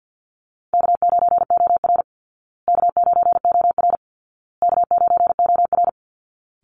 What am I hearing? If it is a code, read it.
Morse code, heard as a keyed tone, about 50 wpm, 700 Hz. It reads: K9OR K9OR K9OR
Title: Call Signs Repeated 2x - 50wpm